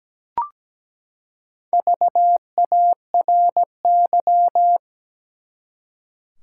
Morse code, heard as a keyed tone, about 17 wpm, 700 Hz. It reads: VARY
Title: Four Letter Words - 17wpm